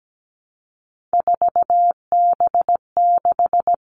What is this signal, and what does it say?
Morse code, heard as a keyed tone, about 17 wpm, 700 Hz. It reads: 4B6
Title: Three Challenging Letter-Number Combinations 17wpm